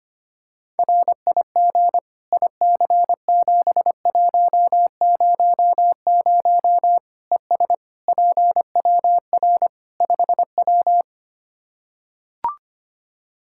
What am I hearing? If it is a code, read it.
Morse code, heard as a keyed tone, about 25 wpm, 700 Hz. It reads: RIG IC7100 ES PWR 5W
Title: Practice QSO - Exchange 2 - Info - 25wpm